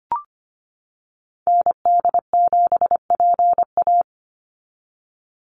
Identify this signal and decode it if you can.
Morse code, heard as a keyed tone, about 25 wpm, 700 Hz. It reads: ND7PA
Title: NA SOTA Activator Callsigns 25wpm